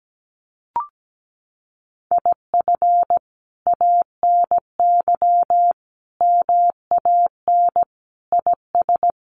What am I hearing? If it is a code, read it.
Morse code, heard as a keyed tone, about 17 wpm, 700 Hz. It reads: IF ANY MAN IS
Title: Sentences from Top 1400 Words 17wpm